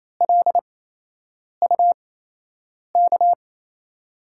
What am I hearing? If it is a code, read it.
Morse code, heard as a keyed tone, about 28 wpm, spaced as for 8 wpm, 700 Hz. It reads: LUK
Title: USA Call Signs - 6 Characters - 8x Character Spacing - 28wpm